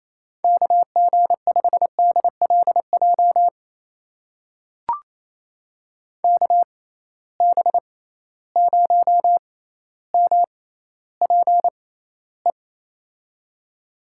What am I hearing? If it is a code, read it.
Morse code, heard as a keyed tone, about 28 wpm, spaced as for 14 wpm, 700 Hz. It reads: KG5DLJ KB0MPE
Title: USA Call Signs - 6 Characters - 6x Character Spacing - 28wpm